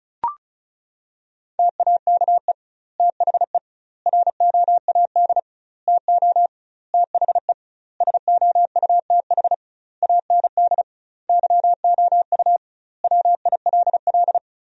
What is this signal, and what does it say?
Morse code, heard as a keyed tone, about 36 wpm, 700 Hz. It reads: TAKE THE ROAD TO THE SOUTH AND YOU WILL
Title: 2x Word-Spacing - Sentences from Top 400 Words - 35wpm